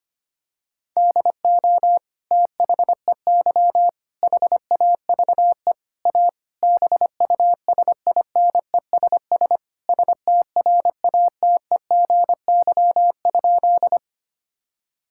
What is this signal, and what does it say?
Morse code, heard as a keyed tone, about 25 wpm, 700 Hz. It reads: DO THEY HAVE A BUSINESS STRATEGY?
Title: Sentences from Top 1000 Words - Review Part 5 - 25wpm